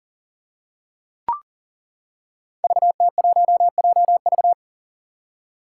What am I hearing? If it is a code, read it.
Morse code, heard as a keyed tone, about 37 wpm, 700 Hz. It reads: VT1JV
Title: Five Challenging Letter-Number Combinations 40wpm